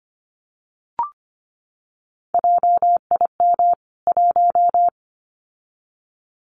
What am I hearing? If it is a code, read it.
Morse code, heard as a keyed tone, about 25 wpm, 700 Hz. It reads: JIM 1
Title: CWT Exchange 25wpm